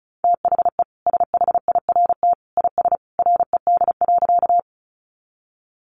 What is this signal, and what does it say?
Morse code, heard as a keyed tone, about 34 wpm, 700 Hz. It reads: THE SHIRT IS RED.
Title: Additional Easy Phrases 35wpm